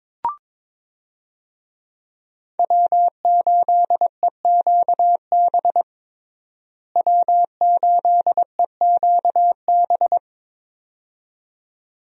Morse code, heard as a keyed tone, about 22 wpm, 700 Hz. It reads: W8EQB W8EQB
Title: USA Call Signs - 5 Characters - Repeated 2x - 22wpm